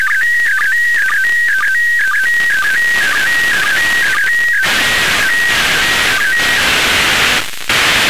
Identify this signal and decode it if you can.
SSTV (slow-scan television), PD 120.
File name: ariss-sstv / ariss- -1445